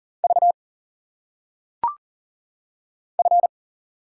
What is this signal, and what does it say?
Morse code, heard as a keyed tone, about 40 wpm, 700 Hz. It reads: V F
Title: Single Letters - Phonetic Pronunciation 40wpm